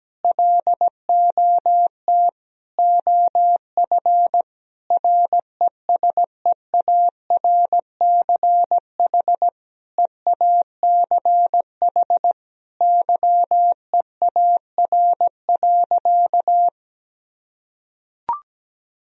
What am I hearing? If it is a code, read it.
Morse code, heard as a keyed tone, about 17 wpm, 700 Hz. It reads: LOT OF RESEARCH EACH YEAR.
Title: Sentences from Top 1000 Words - Review Part 4 - 17wpm